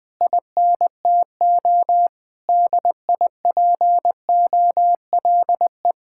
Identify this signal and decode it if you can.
Morse code, heard as a keyed tone, about 20 wpm, 700 Hz. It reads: INTO DIPOLE
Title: Speed-Racing Common QSO Phrases - 20wpm